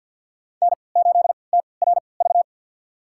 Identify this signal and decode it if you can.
Morse code, heard as a keyed tone, about 50 wpm, spaced as for 28 wpm, 700 Hz. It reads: N8TRV
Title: USA Call Signs - 5 Characters - 3x Character Spacing - 50wpm